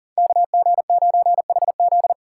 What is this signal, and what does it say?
Morse code, heard as a keyed tone, about 37 wpm, 700 Hz. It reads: KG9HZ
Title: USA Call Signs - 2x Character Spacing - 40wpm